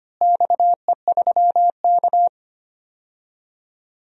Morse code, heard as a keyed tone, about 25 wpm, 700 Hz. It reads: XE3K
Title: DXCC-100 Call Signs - 4 Characters - Repeated 2x - 25wpm